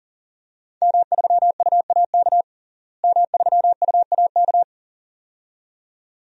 Morse code, heard as a keyed tone, about 40 wpm, 700 Hz. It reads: M3UAK M3UAK
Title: DXCC-100 Call Signs - 5 Characters - Repeated 2x - 40wpm